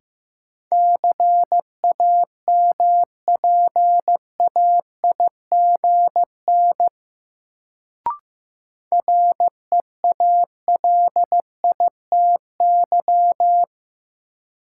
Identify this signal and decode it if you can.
Morse code, heard as a keyed tone, about 15 wpm, 700 Hz. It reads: CAMPAIGN REALITY
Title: Sentences from Top 1200 Words - New Word Drill Alt - 15wpm